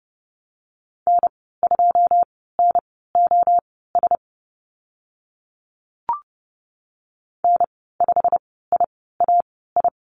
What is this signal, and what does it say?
Morse code, heard as a keyed tone, about 30 wpm, spaced as for 17 wpm, 700 Hz. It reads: N2NOS N5IAI
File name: Lesson 15g - Number 5 - Call Signs 30wpm